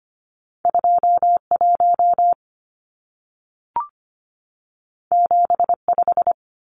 Morse code, heard as a keyed tone, about 25 wpm, 700 Hz. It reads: 2 1 7 5
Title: Numbers - 2-Digit 25wpm